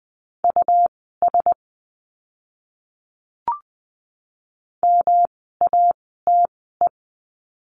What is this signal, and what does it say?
Morse code, heard as a keyed tone, about 20 wpm, spaced as for 15 wpm, 700 Hz. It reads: US MATE
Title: 108 Lesson 18f - Letter m - Words 20wpm